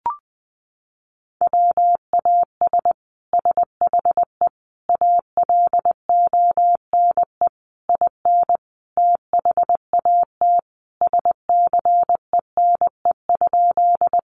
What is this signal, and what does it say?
Morse code, heard as a keyed tone, about 20 wpm, 700 Hz. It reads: WAS SHE ALONE IN THAT SCENE?
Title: Sentences from Top 1100 Words 20wpm